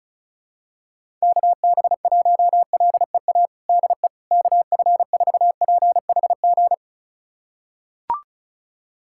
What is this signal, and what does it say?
Morse code, heard as a keyed tone, about 36 wpm, 700 Hz. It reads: KB1LEA DE KF4PHG
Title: USA Callsign DE Callsign 35wpm